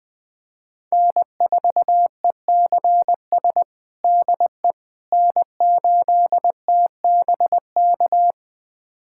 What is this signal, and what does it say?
Morse code, heard as a keyed tone, about 21 wpm, 700 Hz. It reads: N4ECS DE N8TBK
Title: USA Callsign DE Callsign - 5 Characters - 20wpm